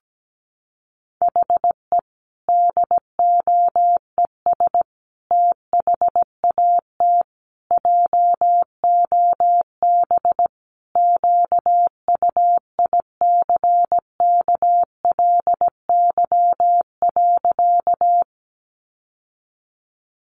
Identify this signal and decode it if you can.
Morse code, heard as a keyed tone, about 17 wpm, 700 Hz. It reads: HE DOES THAT JOB QUICKLY.
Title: Sentences from Top 700 Words 17wpm